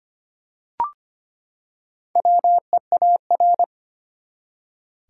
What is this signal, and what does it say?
Morse code, heard as a keyed tone, about 25 wpm, 700 Hz. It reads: WEAR
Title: Four Letter Words - 25wpm